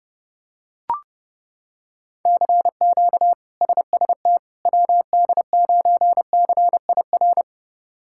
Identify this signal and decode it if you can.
Morse code, heard as a keyed tone, about 30 wpm, 700 Hz. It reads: CQ SST WD9CIR
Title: K1USN SST QSO Practice - 30wpm